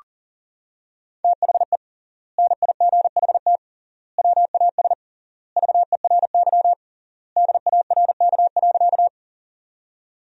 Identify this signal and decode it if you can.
Morse code, heard as a keyed tone, about 40 wpm, 700 Hz. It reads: THE NIGHT WAS VERY DARK.
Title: 3x Word-Spacing - Sentences from Top 500 Words - 40wpm